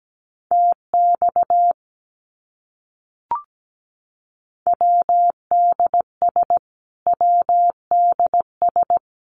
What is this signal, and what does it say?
Morse code, heard as a keyed tone, about 17 wpm, 700 Hz. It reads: TX WDS WDS
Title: Abbreviations Repeated 3x - 17wpm